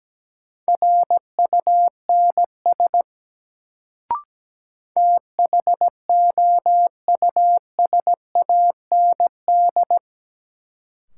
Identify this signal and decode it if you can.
Morse code, heard as a keyed tone, about 17 wpm, 700 Hz. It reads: RUNS THOUSAND
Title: Sentences from Top 600 Words - New Word Drill - 17wpm